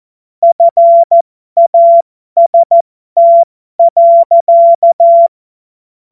Morse code, heard as a keyed tone, about 14 wpm, spaced as for 12 wpm, 680 Hz. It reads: FAST.
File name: Morsekurs / Witze